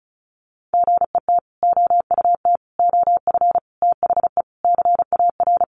Morse code, heard as a keyed tone, about 34 wpm, 700 Hz. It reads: GET OUT OF THE CAR
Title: Sets of 5 Words Encore2 35wpm